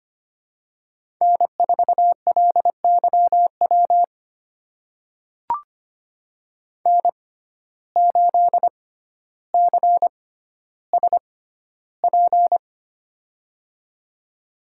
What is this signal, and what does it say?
Morse code, heard as a keyed tone, about 25 wpm, spaced as for 13 wpm, 700 Hz. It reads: N4LYW N8CSP
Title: USA Call Signs - 5 Characters - 6x Character Spacing - 25wpm